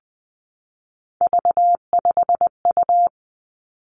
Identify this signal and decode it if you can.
Morse code, heard as a keyed tone, about 20 wpm, 700 Hz. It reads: V5U
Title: Three Challenging Letter-Number Combinations 20wpm